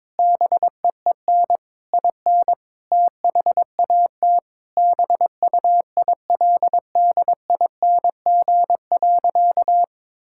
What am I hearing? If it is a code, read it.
Morse code, heard as a keyed tone, about 22 wpm, 700 Hz. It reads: BEEN IN THAT BUILDING.
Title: Sentences from Top 1000 Words - Review Part 5 - 22wpm